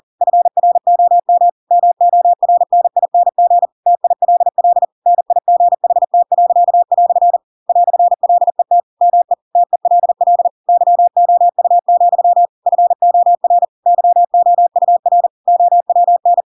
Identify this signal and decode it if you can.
Morse code, heard as a keyed tone, about 40 wpm, 700 Hz. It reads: FROM MORNING TILL NIGHT." "LET ME TELL YOU, FOR YOUR OWN
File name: Pinocchio / Pinocchio ch4 40 40 ik1hge